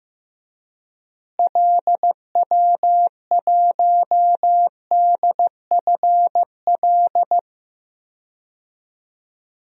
Morse code, heard as a keyed tone, about 15 wpm, 700 Hz. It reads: LW1DFL
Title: DXCC-100 Call Signs - 6 Characters - Repeated 3x - 15wpm